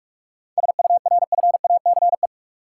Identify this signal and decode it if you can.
Morse code, heard as a keyed tone, about 42 wpm, 700 Hz. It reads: SURFACE
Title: Sets of 4 Words - New Word Drill - 45wpm